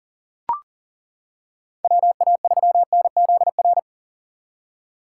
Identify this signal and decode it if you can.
Morse code, heard as a keyed tone, about 40 wpm, 700 Hz. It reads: WA3NZR
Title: K1USN SST QSO Practice - 40wpm